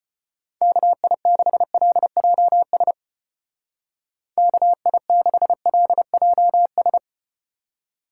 Morse code, heard as a keyed tone, about 34 wpm, 700 Hz. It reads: KI6LJS KI6LJS